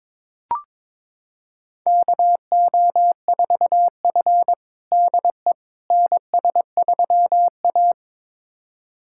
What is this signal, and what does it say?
Morse code, heard as a keyed tone, about 22 wpm, 700 Hz. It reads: KO4F DE NS3A
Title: USA Callsign DE Callsign - 4 Characters - 22wpm